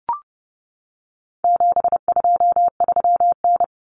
Morse code, heard as a keyed tone, about 30 wpm, 700 Hz. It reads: 723N
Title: Common Cut Numbers - 4-Digit 30wpm